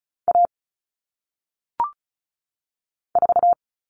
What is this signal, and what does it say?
Morse code, heard as a keyed tone, about 36 wpm, 700 Hz. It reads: A 4
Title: All Cut Numbers - 1-Digit 35wpm